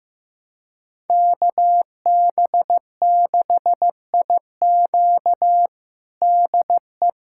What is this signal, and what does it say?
Morse code, heard as a keyed tone, about 15 wpm, 700 Hz. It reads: KB6IQ DE
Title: USA Callsign DE Callsign - 5 Characters - 15wpm